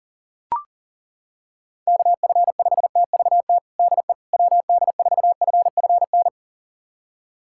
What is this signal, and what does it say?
Morse code, heard as a keyed tone, about 40 wpm, 700 Hz. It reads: KF5TVT DE WD4FFN
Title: USA Callsign DE Callsign - 6 Characters - 40wpm